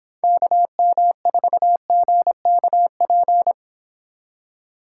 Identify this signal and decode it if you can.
Morse code, heard as a keyed tone, about 26 wpm, 700 Hz. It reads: KM4GKP